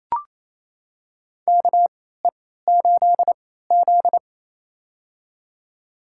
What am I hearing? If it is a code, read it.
Morse code, heard as a keyed tone, about 28 wpm, spaced as for 16 wpm, 700 Hz. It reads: KE8Z
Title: USA Call Signs - 4 Characters - 3x Character Spacing - 28wpm